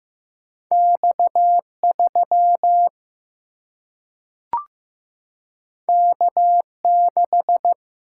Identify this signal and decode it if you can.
Morse code, heard as a keyed tone, about 15 wpm, 700 Hz. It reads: X3 K6
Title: Single-Letter Single-Number 15wpm